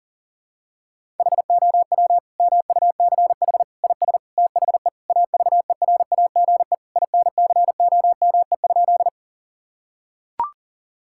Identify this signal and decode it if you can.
Morse code, heard as a keyed tone, about 40 wpm, 700 Hz. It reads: HOW MUCH IS THE AVERAGE INCOME?
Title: Sentences from Top 1000 Words - Review Part 6 - 40wpm